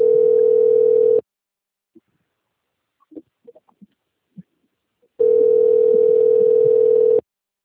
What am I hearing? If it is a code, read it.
Ringback tone (North America), 440 + 480 Hz.
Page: Rio Claro - SP